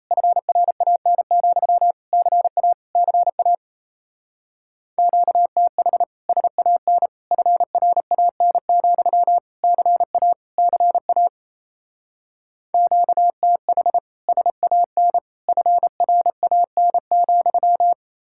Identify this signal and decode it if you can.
Morse code, heard as a keyed tone, about 34 wpm, 700 Hz. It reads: FRAN, CA CA QTH SAN FRAN, CA CA QTH SAN FRAN,
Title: Speed-Racing Common QSO Phrases - 28wpm